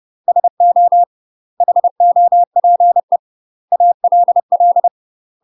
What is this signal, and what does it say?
Morse code, heard as a keyed tone, about 30 wpm, 700 Hz. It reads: SO HOPE ALL
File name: UK MASTER QSOs TRAINING / 30WPM / UK MASTER QSOs 0000